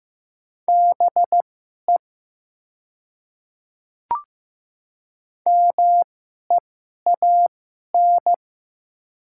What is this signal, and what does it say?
Morse code, heard as a keyed tone, about 15 wpm, spaced as for 11 wpm, 700 Hz. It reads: BE MEAN
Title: 2x Character Spacing - Top 200 Words - 15wpm